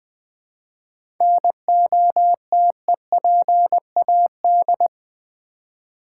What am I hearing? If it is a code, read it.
Morse code, heard as a keyed tone, about 20 wpm, 700 Hz. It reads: NOTEPAD
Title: Easy Words with Hints - 20wpm